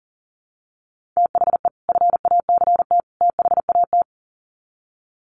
Morse code, heard as a keyed tone, about 40 wpm, 700 Hz. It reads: THE FACT THAT
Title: Sets of 3 Words 40wpm